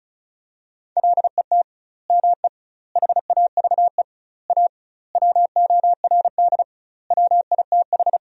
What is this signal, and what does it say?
Morse code, heard as a keyed tone, about 34 wpm, 700 Hz. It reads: LET ME HAVE A WORD WITH
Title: 2x Word-Spacing - Sentences from Top 300 Words - 35wpm